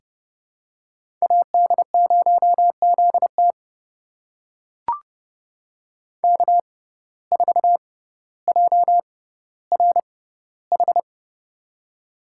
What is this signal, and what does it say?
Morse code, heard as a keyed tone, about 30 wpm, spaced as for 15 wpm, 700 Hz. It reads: AD0ZT K4JRH
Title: USA Call Signs - 5 Characters - 6x Character Spacing - 30wpm